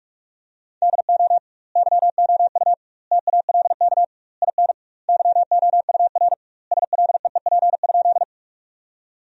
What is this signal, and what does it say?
Morse code, heard as a keyed tone, about 43 wpm, 700 Hz. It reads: DO YOU TALK IN YOUR SLEEP?
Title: 2x Word-Spacing - Sentences from Top 500 Words - 45wpm